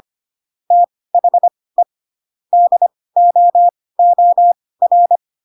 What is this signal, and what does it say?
Morse code, heard as a keyed tone, about 25 wpm, spaced as for 18 wpm, 700 Hz. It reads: THE DOOR
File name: Pinocchio / Pinocchio ch2 12 25 ik1hge